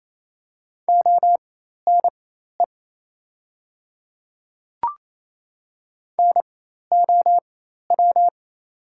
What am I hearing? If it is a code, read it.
Morse code, heard as a keyed tone, about 28 wpm, spaced as for 13 wpm, 700 Hz. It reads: ONE NOW
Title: 4x Character Spacing - Top 100 Words - 28wpm